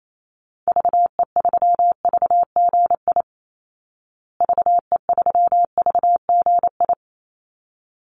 Morse code, heard as a keyed tone, about 28 wpm, 700 Hz. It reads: VE3VGI VE3VGI
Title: Call Signs Repeated 3x - 28wpm